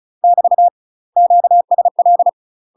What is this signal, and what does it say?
Morse code, heard as a keyed tone, about 36 wpm, 700 Hz. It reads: <BT> QSL
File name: UK MASTER QSOs TRAINING / 35WPM / UK MASTER QSOs 0040